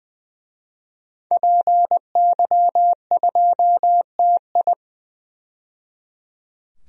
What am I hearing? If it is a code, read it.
Morse code, heard as a keyed tone, about 20 wpm, 700 Hz. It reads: PY2TI
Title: DXCC-100 Call Signs - 5 Characters - 20wpm